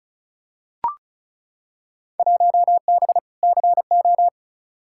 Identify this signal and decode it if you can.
Morse code, heard as a keyed tone, about 34 wpm, 700 Hz. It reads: 1B CO
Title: Field Day Exchanges - 35wpm